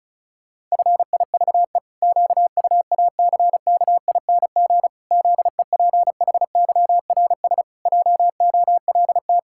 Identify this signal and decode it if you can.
Morse code, heard as a keyed tone, about 36 wpm, 700 Hz. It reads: FIVE QUACKING ZEPHYRS JOLT
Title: Pangram Challenge v2 - 35wpm